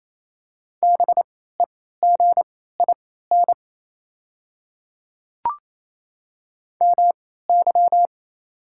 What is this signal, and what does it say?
Morse code, heard as a keyed tone, about 28 wpm, spaced as for 16 wpm, 700 Hz. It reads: BEGIN MY
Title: 3x Character Spacing - Top 500 Words - 28wpm